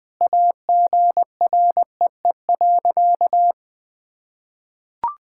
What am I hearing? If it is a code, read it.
Morse code, heard as a keyed tone, about 20 wpm, 700 Hz. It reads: AGREE.
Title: Sentences with 3-Word Phrases - 20wpm